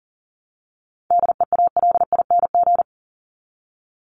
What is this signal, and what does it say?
Morse code, heard as a keyed tone, about 40 wpm, 700 Hz. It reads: DEALING
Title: ING Suffix - Top 100 Words with Base Words 40wpm